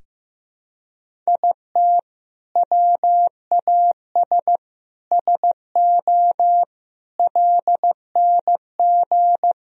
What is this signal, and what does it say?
Morse code, heard as a keyed tone, about 15 wpm, 700 Hz. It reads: IT WAS SO LNG
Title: Sentences from Top 500 Words and Abbreviations 15wpm